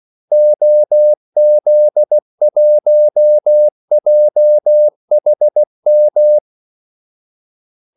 Morse code, heard as a keyed tone, about 16 wpm, 600 Hz. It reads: OZ1JHM
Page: Kaldesignaler 16 wpm